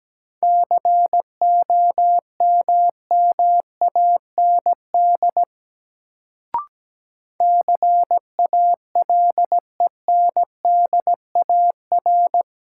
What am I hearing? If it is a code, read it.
Morse code, heard as a keyed tone, about 17 wpm, 700 Hz. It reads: COMMAND CALENDAR
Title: Sentences from Top 1600 Words - New Word Drill - 17wpm